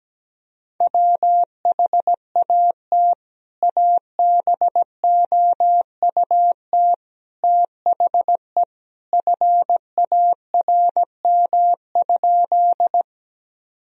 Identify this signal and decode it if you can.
Morse code, heard as a keyed tone, about 17 wpm, 700 Hz. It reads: WHAT ABOUT THE FARM?
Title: Sentences from Top 1000 Words - Review Part 2 - 17wpm